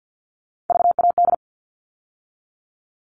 Morse code, heard as a keyed tone, about 50 wpm, 700 Hz. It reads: VAD
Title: All Cut Numbers - 3-Digit 50wpm